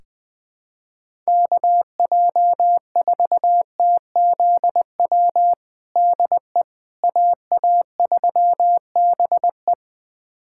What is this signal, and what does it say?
Morse code, heard as a keyed tone, about 20 wpm, 700 Hz. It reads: KJ4TZW DE AA3BE
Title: USA Callsign DE Callsign 20wpm